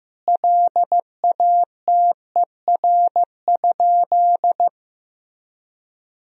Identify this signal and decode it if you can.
Morse code, heard as a keyed tone, about 15 wpm, 700 Hz. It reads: LATER?
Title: Sentences from Top 1000 Words - Review Part 4 - 15wpm